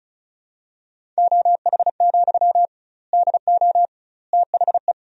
Morse code, heard as a keyed tone, about 36 wpm, 700 Hz. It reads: OH, DO THE
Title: 2x Word-Spacing - Sentences from Top 500 Words - 35wpm